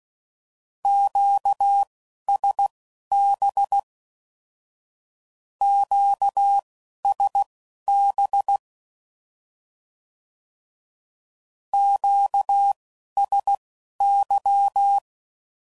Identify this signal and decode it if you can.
Morse code, heard as a keyed tone, about 16 wpm, spaced as for 11 wpm, 800 Hz. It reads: QSB QSB QSY
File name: CW Code Groups @10WPM / Q Calls / Q Calls Twice